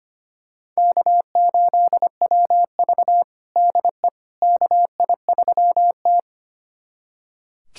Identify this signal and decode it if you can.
Morse code, heard as a keyed tone, about 25 wpm, 700 Hz. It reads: K8WV DE KI3T
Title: USA Callsign DE Callsign - 4 Characters - 25wpm